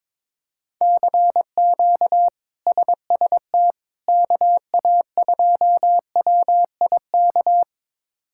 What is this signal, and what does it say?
Morse code, heard as a keyed tone, about 22 wpm, 700 Hz. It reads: CQ SST KA2WIK
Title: K1USN SST QSO Practice - 22wpm